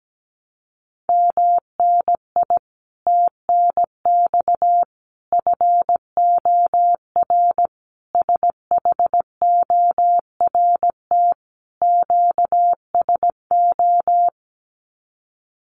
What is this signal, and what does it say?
Morse code, heard as a keyed tone, about 17 wpm, 700 Hz. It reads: MNI TNX FOR SHORT QSO
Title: Common QSO Phrases 17wpm